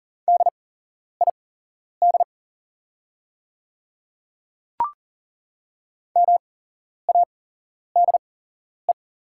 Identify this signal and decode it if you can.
Morse code, heard as a keyed tone, about 40 wpm, spaced as for 11 wpm, 690 Hz. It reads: DID MADE
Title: 8x Character Spacing - Top 300 Words - 40wpm